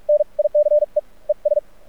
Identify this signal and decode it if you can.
Morse code, heard as a keyed tone, about 43 wpm, spaced as for 36 wpm, 600 Hz. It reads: NICE ES
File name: archived / CRSnet